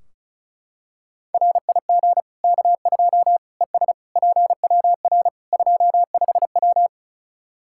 Morse code, heard as a keyed tone, about 34 wpm, 700 Hz. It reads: RIG K2 ES PWR 25W
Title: Practice QSO - Exchange 2 - Info - 35wpm